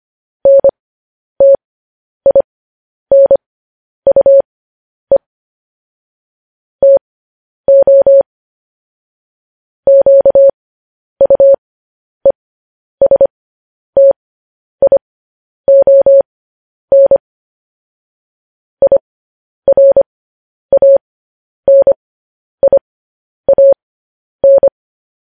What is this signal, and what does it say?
Morse code, heard as a keyed tone, about 25 wpm, spaced as for 10 wpm, 550 Hz. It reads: NTINUE TO QUESTION IRANIAN
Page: news-headlines-in-morse-code-at-10-wpm